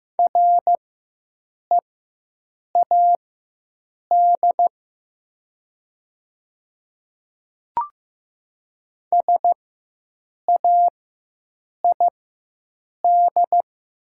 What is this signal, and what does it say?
Morse code, heard as a keyed tone, about 15 wpm, spaced as for 7 wpm, 700 Hz. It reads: READ SAID
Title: 4x Character Spacing - Top 200 Words - 15wpm